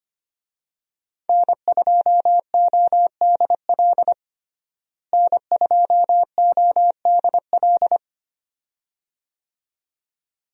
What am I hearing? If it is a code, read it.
Morse code, heard as a keyed tone, about 25 wpm, 700 Hz. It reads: N2ODL N2ODL
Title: USA Call Signs - 5 Characters - Repeated 2x - 25wpm